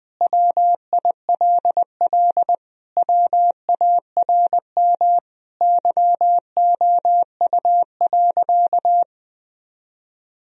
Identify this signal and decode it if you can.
Morse code, heard as a keyed tone, about 20 wpm, 700 Hz. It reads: WILL WARM YOU.
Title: Sentences from Top 1000 Words - Review Part 2 - 20wpm